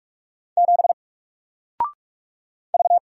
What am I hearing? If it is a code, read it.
Morse code, heard as a keyed tone, about 44 wpm, 700 Hz. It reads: 7 V
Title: Single Letter-Number Alt 45wpm